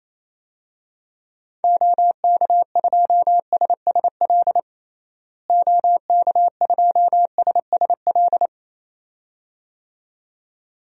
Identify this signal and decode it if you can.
Morse code, heard as a keyed tone, about 28 wpm, 700 Hz. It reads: OK2SSL OK2SSL
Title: DXCC-100 Call Signs - 6 Characters - Repeated 2x - 28wpm